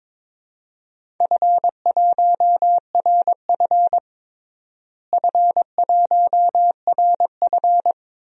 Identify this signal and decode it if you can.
Morse code, heard as a keyed tone, about 22 wpm, 700 Hz. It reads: F1RF F1RF
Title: DXCC-100 Call Signs - 4 Characters - Repeated 2x - 22wpm